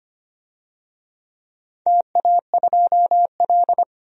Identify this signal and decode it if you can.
Morse code, heard as a keyed tone, about 25 wpm, 700 Hz. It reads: TA2L
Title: DXCC-100 Call Signs - 4 Characters - 25wpm